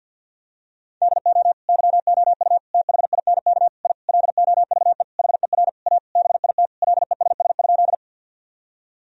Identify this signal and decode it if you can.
Morse code, heard as a keyed tone, about 45 wpm, 700 Hz. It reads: DO YOU THINK I LOVE HER A BIT LESS?
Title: Sentences from Top 1000 Words - Review Part 2 - 50wpm